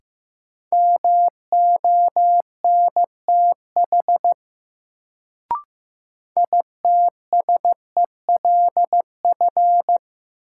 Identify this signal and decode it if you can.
Morse code, heard as a keyed tone, about 15 wpm, 700 Hz. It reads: MONTH ITSELF
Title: Sentences from Top 800 Words - New Word Drill - 15wpm